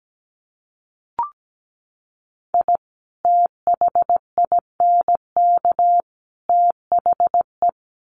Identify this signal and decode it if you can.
Morse code, heard as a keyed tone, about 17 wpm, 700 Hz. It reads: I THINK THE
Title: Sets of 3 Words 17wpm